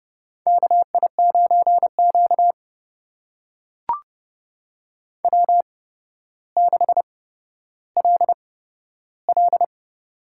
Morse code, heard as a keyed tone, about 32 wpm, spaced as for 13 wpm, 700 Hz. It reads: KI9Q W6LL
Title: USA Call Signs - 4 Characters - 8x Character Spacing - 30wpm